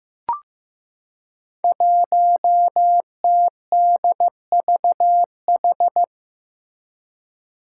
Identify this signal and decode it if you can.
Morse code, heard as a keyed tone, about 15 wpm, 700 Hz. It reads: 1TDVH
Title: Five Challenging Letter-Number Combinations 15wpm